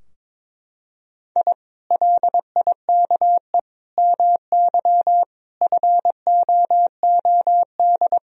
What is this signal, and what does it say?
Morse code, heard as a keyed tone, about 21 wpm, 700 Hz. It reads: I LIKE MY FOOD
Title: Sentences from Top 1000 Words - Review Part 3 - 22wpm